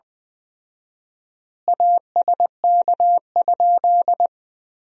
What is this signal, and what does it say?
Morse code, heard as a keyed tone, about 20 wpm, 700 Hz. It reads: ASK?
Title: 4x Word-Spacing - Sentences from Top 200 Words - 20wpm